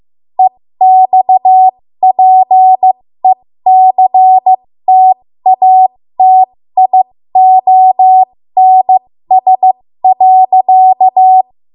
Morse code, heard as a keyed tone, about 15 wpm, spaced as for 13 wpm, 750 Hz. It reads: EXPECTATIONS.
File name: Morse / 13WPM